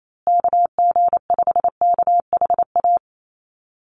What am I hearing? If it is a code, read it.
Morse code, heard as a keyed tone, about 28 wpm, 700 Hz. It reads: KG5KHA
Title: USA Call Signs Repeated 3x 28wpm